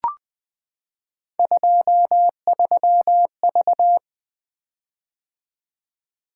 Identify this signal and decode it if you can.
Morse code, heard as a keyed tone, about 20 wpm, 700 Hz. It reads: 23V
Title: Three Letter-Number Combinations - 20wpm